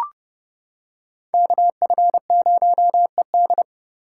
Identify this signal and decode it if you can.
Morse code, heard as a keyed tone, about 28 wpm, 700 Hz. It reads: KF0ED